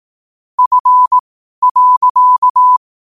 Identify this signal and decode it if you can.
Morse code, heard as a keyed tone, about 18 wpm, spaced as for 12 wpm, 1000 Hz. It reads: F.
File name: qotdfiles / 12wpm